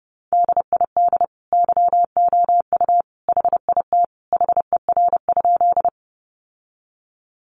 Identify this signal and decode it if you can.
Morse code, heard as a keyed tone, about 30 wpm, 700 Hz. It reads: DID YOU HIT HER?
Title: Sentences from Top 1200 Words 30wpm